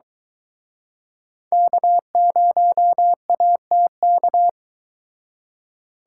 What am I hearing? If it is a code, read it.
Morse code, heard as a keyed tone, about 23 wpm, 700 Hz. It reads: K0ATK